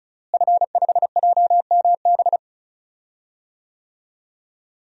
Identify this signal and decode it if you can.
Morse code, heard as a keyed tone, about 34 wpm, 700 Hz. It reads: F5JMB
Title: DXCC-100 Call Signs - 5 Characters - Repeated 2x - 35wpm